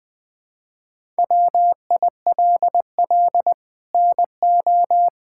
Morse code, heard as a keyed tone, about 21 wpm, 700 Hz. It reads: WILL NO
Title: Sets of 6 Words - 20wpm